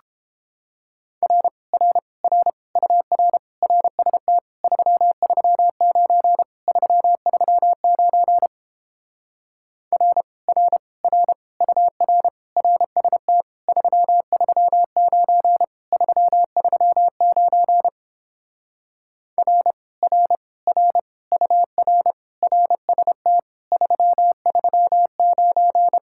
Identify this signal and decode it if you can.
Morse code, heard as a keyed tone, about 29 wpm, 700 Hz. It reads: R R R UR RST 339 339 R R R UR RST 339 339 R R R UR RST 339